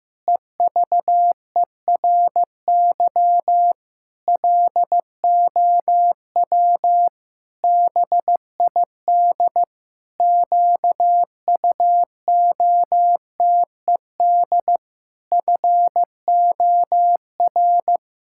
Morse code, heard as a keyed tone, about 15 wpm, 700 Hz. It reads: EVERY LOW BID QUOTED FOR
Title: Pangram Challenge v2 - 15wpm